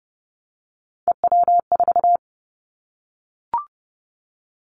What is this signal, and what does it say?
Morse code, heard as a keyed tone, about 30 wpm, 700 Hz. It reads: EW4
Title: Callsign Prefixes for Top 100 DXCC - 30wpm